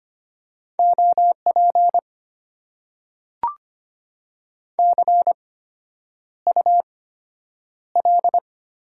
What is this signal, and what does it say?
Morse code, heard as a keyed tone, about 25 wpm, spaced as for 9 wpm, 700 Hz. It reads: OP CUL
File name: Top 90 QSO Elements - 8x Character Spacing - 25wpm